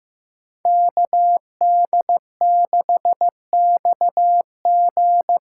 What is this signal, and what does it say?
Morse code, heard as a keyed tone, about 16 wpm, 700 Hz. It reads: KD6XG
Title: USA Callsign DE Callsign - 5 Characters - 15wpm